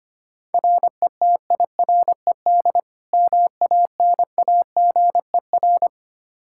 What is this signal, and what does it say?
Morse code, heard as a keyed tone, about 25 wpm, 700 Hz. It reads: RETIRED MANAGER
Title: K1USN SST QSO Practice 25wpm